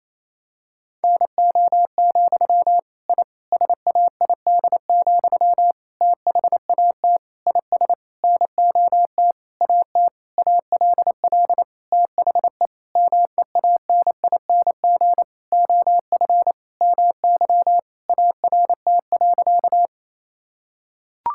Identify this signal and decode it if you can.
Morse code, heard as a keyed tone, about 28 wpm, 700 Hz. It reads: NO, I SAID, THAT IS NOT AT ALL THE MEANING OF MY ART.
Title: Sentences from Top 1000 Words - Review Part 6 - 28wpm